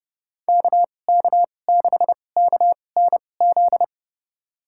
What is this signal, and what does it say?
Morse code, heard as a keyed tone, about 31 wpm, spaced as for 22 wpm, 700 Hz. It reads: KK6KNZ
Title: USA Call Signs - 6 Characters - 2x Character Spacing - 30wpm